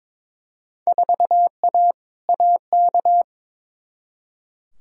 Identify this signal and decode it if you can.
Morse code, heard as a keyed tone, about 22 wpm, 700 Hz. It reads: 4A AK
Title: Field Day Exchanges Encore - 22wpm